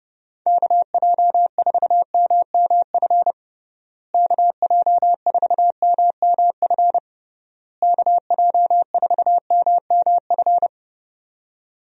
Morse code, heard as a keyed tone, about 30 wpm, 700 Hz. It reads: KJ4MMF KJ4MMF KJ4MMF
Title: USA Call Signs - 6 Characters - Repeated 3x - 30wpm